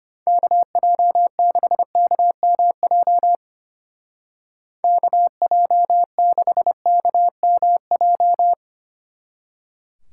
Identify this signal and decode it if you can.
Morse code, heard as a keyed tone, about 30 wpm, 700 Hz. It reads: KJ6KMJ KJ6KMJ
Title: Speed Racing - USA 2x3 Call Signs 25wpm